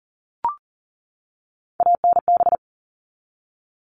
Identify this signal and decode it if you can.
Morse code, heard as a keyed tone, about 40 wpm, 700 Hz. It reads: ANB
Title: All Cut Numbers - 3-Digit 40wpm